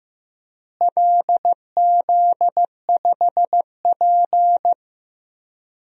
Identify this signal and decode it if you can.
Morse code, heard as a keyed tone, about 15 wpm, 700 Hz. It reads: LZ5P
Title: DXCC-100 Call Signs - 4 Characters - Repeated 2x - 15wpm